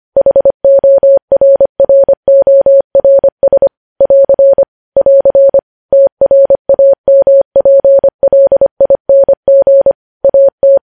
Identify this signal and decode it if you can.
Morse code, heard as a keyed tone, about 25 wpm, 550 Hz. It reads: HORRORS <AR> <AR> TRAMPLING AT
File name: podcast / News headlines-1x-25x25